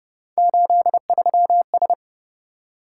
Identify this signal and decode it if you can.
Morse code, heard as a keyed tone, about 30 wpm, 700 Hz. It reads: 83S